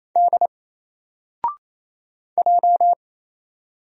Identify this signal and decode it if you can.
Morse code, heard as a keyed tone, about 27 wpm, 700 Hz. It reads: D J
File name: Single-Letter-Number-28wpm